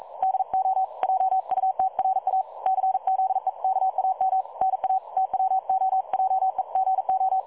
Morse code, heard as a keyed tone, about 42 wpm, 760 Hz. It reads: DO JUTRA CZEKAM NA TWOJEGO
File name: dlazaawansowanych / nagraniazpasma / websdr recording 3565.0kHz